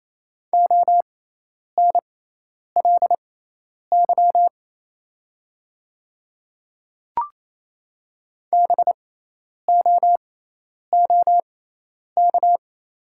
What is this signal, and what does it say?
Morse code, heard as a keyed tone, about 28 wpm, spaced as for 10 wpm, 690 Hz. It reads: ONLY BOOK
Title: 6x Character Spacing - Top 300 Words - 28wpm